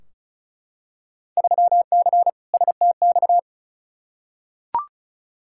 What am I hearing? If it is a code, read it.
Morse code, heard as a keyed tone, about 34 wpm, 700 Hz. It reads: 3C STX
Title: Field Day Exchanges Encore - 35wpm